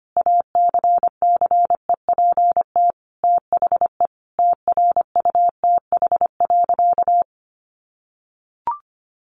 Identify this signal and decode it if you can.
Morse code, heard as a keyed tone, about 25 wpm, 700 Hz. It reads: ACCEPT THE TRUTH.
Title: Sentences from Top 1300 Words 25wpm